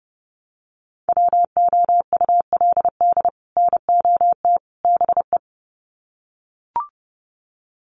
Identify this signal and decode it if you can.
Morse code, heard as a keyed tone, about 30 wpm, 700 Hz. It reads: WOULD NOT BE
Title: Sets of 3 Words 30wpm